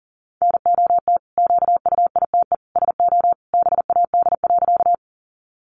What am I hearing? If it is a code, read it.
Morse code, heard as a keyed tone, about 40 wpm, 700 Hz. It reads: NOT QUITE SO BAD.
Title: Sentences from Top 700 Words 40wpm